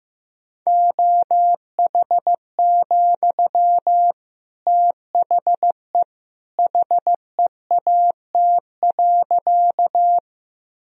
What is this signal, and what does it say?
Morse code, heard as a keyed tone, about 15 wpm, 700 Hz. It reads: OH, THE HEAT.
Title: Sentences from Top 1000 Words - Review Part 3 - 15wpm